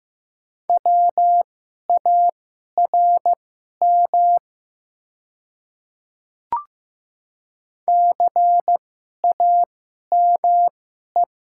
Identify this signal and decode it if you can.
Morse code, heard as a keyed tone, about 15 wpm, spaced as for 11 wpm, 700 Hz. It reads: WARM CAME
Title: 2x Character Spacing - Top 500 Words - 15wpm